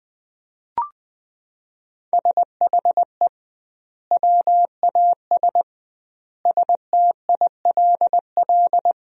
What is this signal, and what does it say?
Morse code, heard as a keyed tone, about 20 wpm, 700 Hz. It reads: SHE WAS STILL
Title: 2x Word-Spacing - Sentences from Top 500 Words - 20wpm